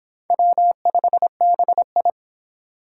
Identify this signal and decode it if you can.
Morse code, heard as a keyed tone, about 25 wpm, 700 Hz. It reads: W5BI